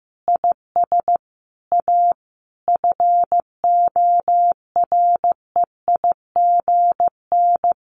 Morse code, heard as a keyed tone, about 15 wpm, 700 Hz. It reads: IS A FOREIGN
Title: Sentences from Top 1400 Words 15wpm